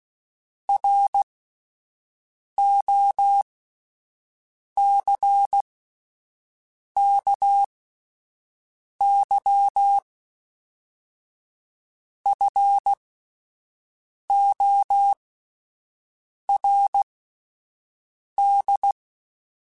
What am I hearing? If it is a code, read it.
Morse code, heard as a keyed tone, about 16 wpm, spaced as for 5 wpm, 800 Hz. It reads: ROCKYFORD